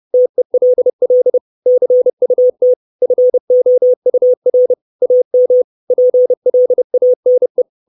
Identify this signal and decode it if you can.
Morse code, heard as a keyed tone, about 30 wpm, 500 Hz. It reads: TELL CUT FOUR AM PLANE
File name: 632 Words 30WPM